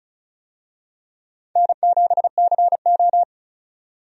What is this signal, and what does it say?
Morse code, heard as a keyed tone, about 36 wpm, 700 Hz. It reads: N7CO
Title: USA Call Signs - 4 Characters - 35wpm